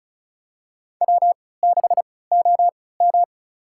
Morse code, heard as a keyed tone, about 36 wpm, spaced as for 20 wpm, 700 Hz. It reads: W6OM
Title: USA Call Signs - 4 Characters - 3x Character Spacing - 35wpm